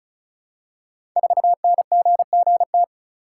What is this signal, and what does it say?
Morse code, heard as a keyed tone, about 36 wpm, 690 Hz. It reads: 4NGGT
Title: Code Groups - 5 Letter:Numbers - 35wpm